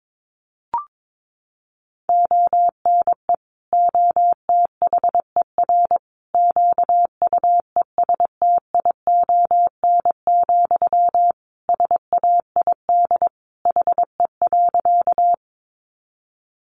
Morse code, heard as a keyed tone, about 22 wpm, 700 Hz. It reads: ONE OTHER QUESTION, SAID HE.
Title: Sentences from Top 400 Words 22wpm